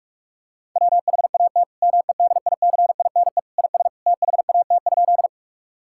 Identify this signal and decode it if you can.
Morse code, heard as a keyed tone, about 46 wpm, 700 Hz. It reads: WHAT MEDICINE IS THAT?
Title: Sentences from Top 1000 Words - Review Part 6 - 45wpm